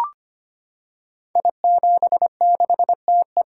Morse code, heard as a keyed tone, about 25 wpm, 700 Hz. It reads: I76TE
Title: Code Groups - 5 Letter:Numbers - 25wpm